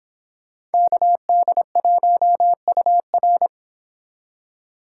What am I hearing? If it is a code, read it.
Morse code, heard as a keyed tone, about 30 wpm, 700 Hz. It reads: KD1UR